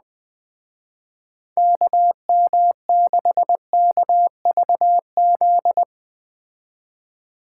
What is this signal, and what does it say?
Morse code, heard as a keyed tone, about 20 wpm, 700 Hz. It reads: KM6KVZ
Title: Speed Racing - USA 2x3 Call Signs 17wpm